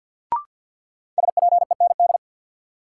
Morse code, heard as a keyed tone, about 53 wpm, 700 Hz. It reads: SPEND
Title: Sets of 5 Words - New Word Drill - 50wpm